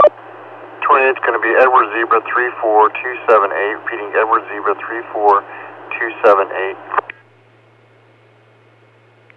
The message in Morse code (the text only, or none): none